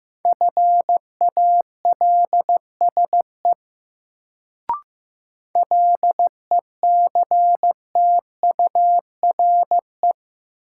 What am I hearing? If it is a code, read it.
Morse code, heard as a keyed tone, about 15 wpm, 700 Hz. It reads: FALSE LECTURE
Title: Sentences from Top 1600 Words - New Word Drill - 15wpm